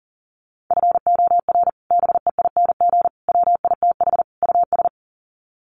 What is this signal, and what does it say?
Morse code, heard as a keyed tone, about 40 wpm, 700 Hz. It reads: FOR BEING WITH US
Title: Sets of 4 Words 40wpm